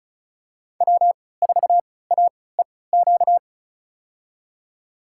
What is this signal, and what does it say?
Morse code, heard as a keyed tone, about 36 wpm, spaced as for 20 wpm, 700 Hz. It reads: W4AEQ
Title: USA Call Signs - 5 Characters - 3x Character Spacing - 35wpm